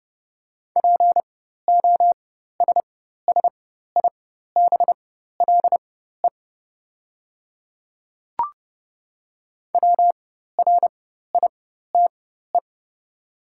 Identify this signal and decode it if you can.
Morse code, heard as a keyed tone, about 30 wpm, spaced as for 14 wpm, 700 Hz. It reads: POSSIBLE WRITE
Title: 4x Character Spacing - Top 500 Words - 30wpm